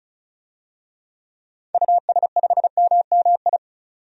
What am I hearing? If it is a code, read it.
Morse code, heard as a keyed tone, about 36 wpm, 700 Hz. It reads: US5MMI
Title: DXCC-100 Call Signs - 6 Characters - 35wpm